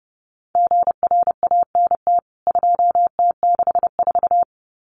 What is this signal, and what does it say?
Morse code, heard as a keyed tone, about 30 wpm, 700 Hz. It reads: GRANT 2T64
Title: CWT Exchange 30wpm